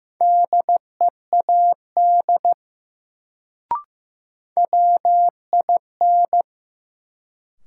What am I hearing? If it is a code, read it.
Morse code, heard as a keyed tone, about 15 wpm, 700 Hz. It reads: DEAD WIN
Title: Sentences from Top 1500 Words - New Word Drill Alt - 15wpm